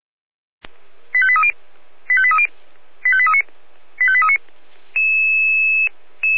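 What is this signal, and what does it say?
Five-tone selective call (ZVEI), heard as quick runs of five tones, tones of about 70 ms. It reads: 85830 85830 85839 85839
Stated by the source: Five-tone ZVEI selective call received on an SDR.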